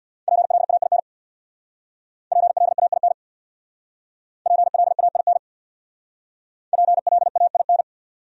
Morse code, heard as a keyed tone, about 62 wpm, 700 Hz. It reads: PLAIN PLAIN PLAIN PLAIN
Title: Speed-Racing Top 500 Words 50wpm